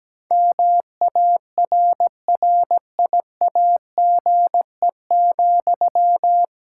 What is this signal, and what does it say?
Morse code, heard as a keyed tone, about 17 wpm, 700 Hz. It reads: MARRIAGE,
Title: Sentences from Top 1000 Words - Review Part 5 - 17wpm